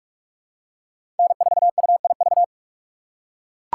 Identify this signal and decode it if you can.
Morse code, heard as a keyed tone, about 41 wpm, 690 Hz. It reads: N4UIV
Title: USA Call Signs - 5 Characters - 6x Character Spacing - 45wpm